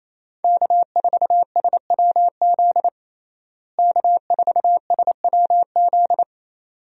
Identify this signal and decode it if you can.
Morse code, heard as a keyed tone, about 28 wpm, 700 Hz. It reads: K4SWZ K4SWZ
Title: USA Call Signs - 5 Characters - Repeated 2x - 28wpm